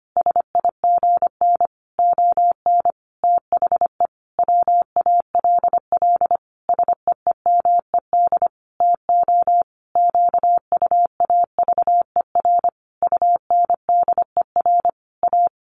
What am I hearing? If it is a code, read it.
Morse code, heard as a keyed tone, about 25 wpm, 700 Hz. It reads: SIGN ON THE WALL SEEMED TO QUAVER UNDER A
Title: A Sound Of Thunder 25wpm